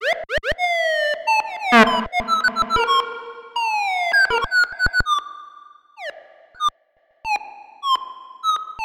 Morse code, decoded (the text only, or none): none